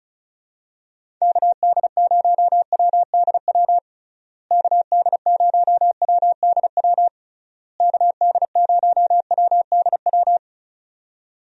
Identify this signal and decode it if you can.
Morse code, heard as a keyed tone, about 34 wpm, 700 Hz. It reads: KD0WDW KD0WDW KD0WDW
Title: USA Call Signs - 6 Characters - Repeated 3x - 35wpm